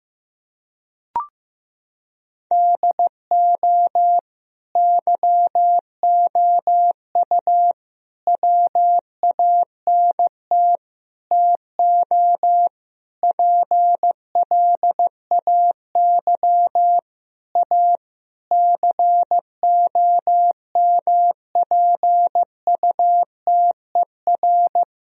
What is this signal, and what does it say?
Morse code, heard as a keyed tone, about 15 wpm, 700 Hz. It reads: DO YOU WANT TO PLAY A COMPUTER
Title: Sentences from Top 1000 Words - Review Part 3 - 15wpm